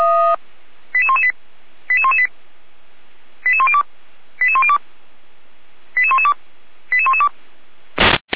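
Five-tone selective call (ZVEI), heard as quick runs of five tones, tones of about 70 ms. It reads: 88118 88118 88102 88102 88112 88112
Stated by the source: ZVEI five-tone call received by SDR.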